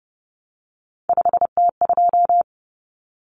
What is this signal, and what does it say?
Morse code, heard as a keyed tone, about 30 wpm, 700 Hz. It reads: 5T2
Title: Common Cut Numbers - 3-Digit 30wpm